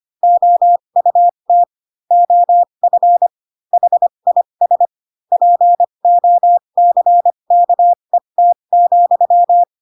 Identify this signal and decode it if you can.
Morse code, heard as a keyed tone, about 25 wpm, spaced as for 22 wpm, 700 Hz. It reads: OUT OF HIS POCKET,
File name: Pinocchio / Pinocchio ch7 18 25 ik1hge